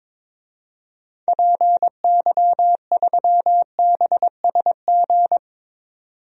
Morse code, heard as a keyed tone, about 22 wpm, 700 Hz. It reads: PY3BSG
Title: DXCC-100 Call Signs - 6 Characters - 22wpm